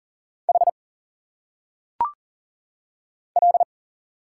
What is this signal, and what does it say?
Morse code, heard as a keyed tone, about 40 wpm, 700 Hz. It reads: H L
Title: Single Letters - Phonetic Pronunciation 40wpm